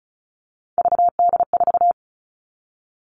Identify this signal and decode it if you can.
Morse code, heard as a keyed tone, about 36 wpm, 700 Hz. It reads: VD4
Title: Three Challenging Letter-Number Combinations 35wpm